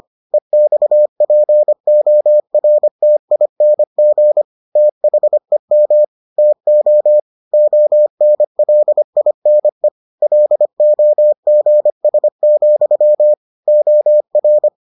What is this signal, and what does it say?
Morse code, heard as a keyed tone, about 25 wpm, 600 Hz. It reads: EXPORTING THEM TO ONLINE LOGS, OR